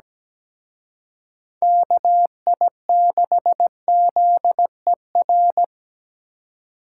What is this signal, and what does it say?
Morse code, heard as a keyed tone, about 20 wpm, 700 Hz. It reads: KI6ZER
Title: Speed Racing - USA 2x3 Call Signs 17wpm